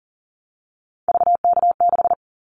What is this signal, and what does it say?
Morse code, heard as a keyed tone, about 40 wpm, 700 Hz. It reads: VK6
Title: Callsign Prefixes for Top 50 DXCC - 40wpm